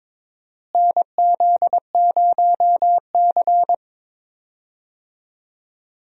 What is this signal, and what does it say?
Morse code, heard as a keyed tone, about 22 wpm, 700 Hz. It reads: NZ0C
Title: USA Call Signs - 4 Characters - Repeated 2x - 22wpm